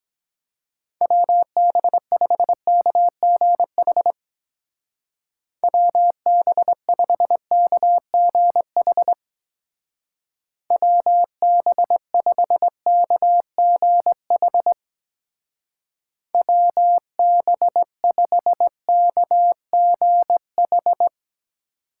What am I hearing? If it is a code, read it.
Morse code, heard as a keyed tone, about 22 wpm, 700 Hz. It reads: WB5KGH WB5KGH WB5KGH WB5KGH